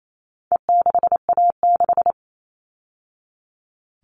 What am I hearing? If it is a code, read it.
Morse code, heard as a keyed tone, about 28 wpm, 700 Hz. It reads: E6A6
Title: All Cut Numbers - 4-Digit 28wpm